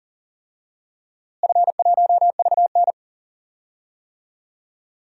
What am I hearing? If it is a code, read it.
Morse code, heard as a keyed tone, about 40 wpm, 700 Hz. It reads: F1VN
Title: DXCC-100 Call Signs - 4 Characters - Repeated 3x - 40wpm